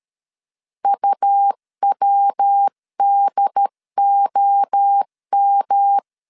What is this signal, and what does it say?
Morse code, heard as a keyed tone, about 13 wpm, 800 Hz. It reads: UWDOM
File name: CW / DET / 1 3 1500C S A 12wpm 2905